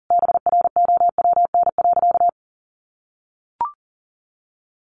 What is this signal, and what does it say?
Morse code, heard as a keyed tone, about 40 wpm, 700 Hz. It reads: BROWN.
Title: Sentences from Top 1200 Words 40wpm